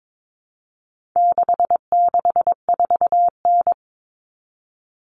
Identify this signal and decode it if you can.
Morse code, heard as a keyed tone, about 22 wpm, 700 Hz. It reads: 664N
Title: Common Cut Numbers - 4-Digit 22wpm